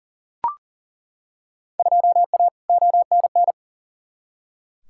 Morse code, heard as a keyed tone, about 40 wpm, 700 Hz. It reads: 2A ONN
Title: Field Day Exchanges Encore - 40wpm